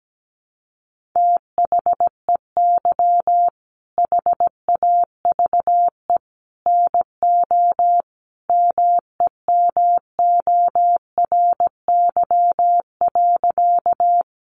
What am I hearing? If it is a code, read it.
Morse code, heard as a keyed tone, about 17 wpm, 700 Hz. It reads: THEY HAVE NO MEMORY.
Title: Sentences from Top 1300 Words 17wpm